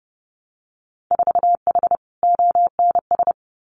Morse code, heard as a keyed tone, about 30 wpm, 700 Hz. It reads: 4H ONS
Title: Winter Field Day Exchanges Encore 30wpm